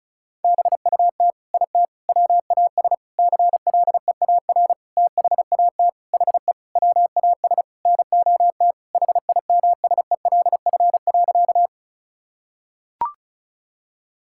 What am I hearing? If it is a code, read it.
Morse code, heard as a keyed tone, about 36 wpm, 700 Hz. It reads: BUT IT WAS CLEAR THAT HE WAS NOT HIMSELF.
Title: Sentences from Top 1600 Words 35wpm